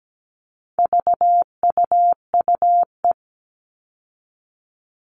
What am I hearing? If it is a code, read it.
Morse code, heard as a keyed tone, about 17 wpm, 700 Hz. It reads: VUUE
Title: All Cut Numbers - 4-Digit 17wpm